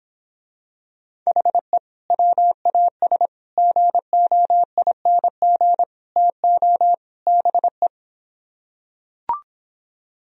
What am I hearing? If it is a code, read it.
Morse code, heard as a keyed tone, about 26 wpm, 700 Hz. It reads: HE WAS GOING TO BE